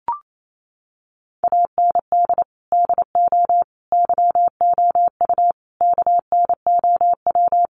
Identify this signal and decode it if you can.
Morse code, heard as a keyed tone, about 28 wpm, 700 Hz. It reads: AND DO YOU KNOW
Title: Sentences from Top 200 Words 28wpm